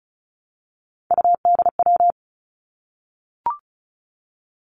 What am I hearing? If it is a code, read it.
Morse code, heard as a keyed tone, about 36 wpm, 700 Hz. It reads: UDW
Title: Three Challenging Letter-Number Combinations 35wpm